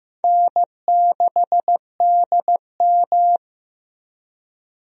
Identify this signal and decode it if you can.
Morse code, heard as a keyed tone, about 15 wpm, 700 Hz. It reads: N6DM
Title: USA Call Signs - 4 Characters - Repeated 3x - 15wpm